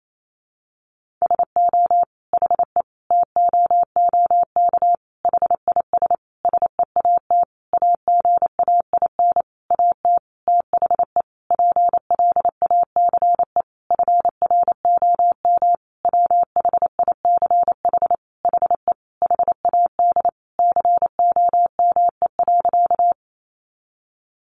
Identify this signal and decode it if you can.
Morse code, heard as a keyed tone, about 28 wpm, 700 Hz. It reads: SO HE TOOK HIS SEAT AGAIN AT THE PLACE FROM WHICH HE HAD COME.
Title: Sentences from Top 1500 Words 28wpm